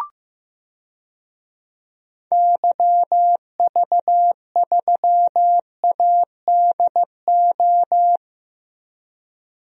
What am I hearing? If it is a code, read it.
Morse code, heard as a keyed tone, about 15 wpm, 700 Hz. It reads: YV3ADO
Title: DXCC-100 Call Signs - 6 Characters - Repeated 3x - 15wpm